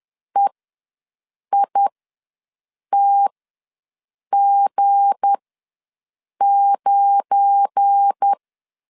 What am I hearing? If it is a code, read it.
Morse code, heard as a keyed tone, about 10 wpm, spaced as for 6 wpm, 800 Hz. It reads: EITG9
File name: LEZIONI / 7 GF9Z / 7 GF9 ws 10wpm 4034